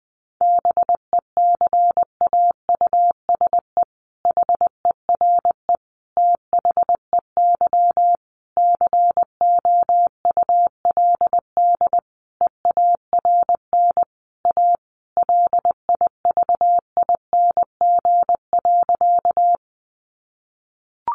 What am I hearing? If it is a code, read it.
Morse code, heard as a keyed tone, about 20 wpm, 700 Hz. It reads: BECAUSE HERE THEY COULD EARN A LIVING.
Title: Sentences from Top 1500 Words 20wpm